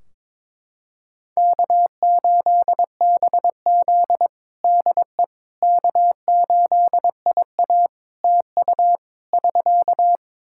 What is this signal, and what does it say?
Morse code, heard as a keyed tone, about 22 wpm, 700 Hz. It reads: K8BZ DE K8IA TU <SK>
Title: K1USN SST QSO Practice 22wpm